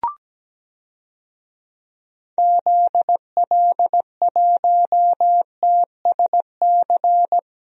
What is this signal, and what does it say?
Morse code, heard as a keyed tone, about 17 wpm, 700 Hz. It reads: ZL1TSC
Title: DXCC-100 Call Signs - 6 Characters - Repeated 3x - 17wpm